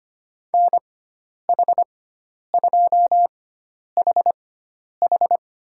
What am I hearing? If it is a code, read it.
Morse code, heard as a keyed tone, about 25 wpm, spaced as for 10 wpm, 700 Hz. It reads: NH2HH
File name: Lesson 16g - Letter c - Call Signs 25wpm